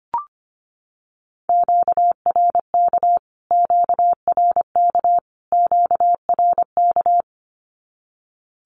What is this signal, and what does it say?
Morse code, heard as a keyed tone, about 25 wpm, 700 Hz. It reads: QRK QRK QRK
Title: Q-Codes Repeated 3x 25wpm